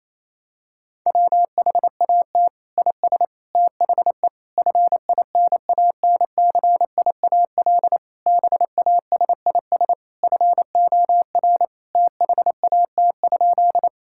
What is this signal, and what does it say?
Morse code, heard as a keyed tone, about 28 wpm, 700 Hz. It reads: WHAT IS THE FINANCIAL BASIS FOR THAT?
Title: Sentences from Top 1000 Words - Review Part 5 - 28wpm